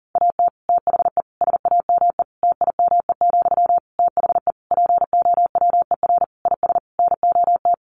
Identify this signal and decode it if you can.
Morse code, heard as a keyed tone, about 40 wpm, 700 Hz. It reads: AT THE SAME TIME, THE POWER IS NOT
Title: Sentences from Top 1500 Words 40wpm